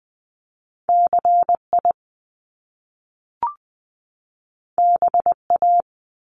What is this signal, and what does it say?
Morse code, heard as a keyed tone, about 20 wpm, 700 Hz. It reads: CI BA
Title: Top Two Letter Combinations in Words 20wpm